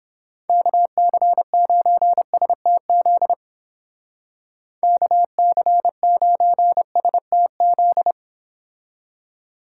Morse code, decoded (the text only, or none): KC9STZ KC9STZ